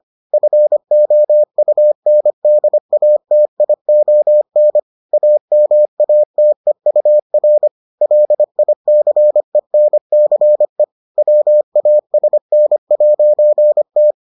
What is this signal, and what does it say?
Morse code, heard as a keyed tone, about 25 wpm, 600 Hz. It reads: FOUNDATION AMATEUR LICENCE WASN'T